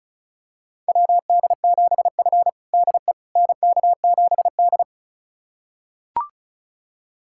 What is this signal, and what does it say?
Morse code, heard as a keyed tone, about 36 wpm, 700 Hz. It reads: WD7F DE NK7D
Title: USA Callsign DE Callsign - 4 Characters - 35wpm